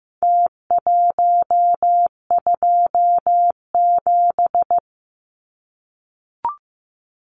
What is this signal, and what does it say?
Morse code, heard as a keyed tone, about 15 wpm, 700 Hz. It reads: T127
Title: Numbers - Contest Style - 4-Digit 15wpm